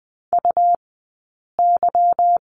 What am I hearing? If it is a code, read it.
Morse code, heard as a keyed tone, about 20 wpm, 700 Hz. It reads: U Y
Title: 158 Lesson 26d - Letter g - Two Characters 20wpm